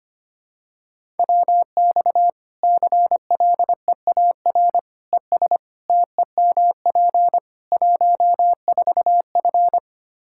Morse code, heard as a keyed tone, about 25 wpm, 700 Hz. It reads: WX CLEAR ES TEMP 14F
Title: K1USN SST QSO Practice 25wpm